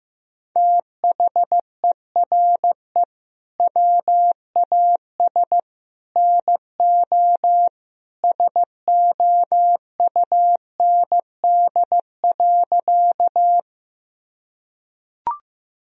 Morse code, heard as a keyed tone, about 15 wpm, 700 Hz. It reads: THERE WAS NO SOUND.
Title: Sentences from Top 1000 Words - Review Part 1 - 15wpm